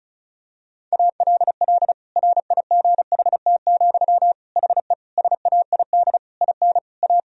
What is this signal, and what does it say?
Morse code, heard as a keyed tone, about 36 wpm, 700 Hz. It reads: ALL RIGHT, HE SAID IN A
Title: Sentences from Top 1000 Words - Review Part 1 - 35wpm